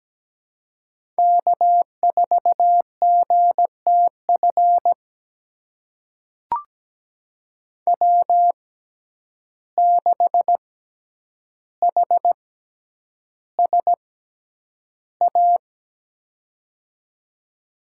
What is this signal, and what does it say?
Morse code, heard as a keyed tone, about 17 wpm, spaced as for 9 wpm, 700 Hz. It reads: K4GTF W6HSA
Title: USA Call Signs - 5 Characters - 6x Character Spacing - 17wpm